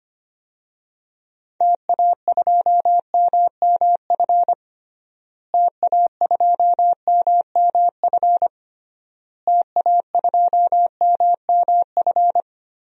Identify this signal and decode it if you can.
Morse code, heard as a keyed tone, about 25 wpm, 700 Hz. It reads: TA2MMF TA2MMF TA2MMF
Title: DXCC-100 Call Signs - 6 Characters - Repeated 3x - 25wpm